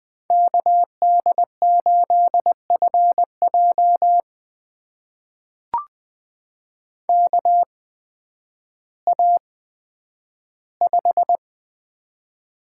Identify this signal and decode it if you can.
Morse code, heard as a keyed tone, about 20 wpm, spaced as for 11 wpm, 700 Hz. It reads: KD8FJ KA5
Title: USA Call Signs - 8x Character Spacing - 20wpm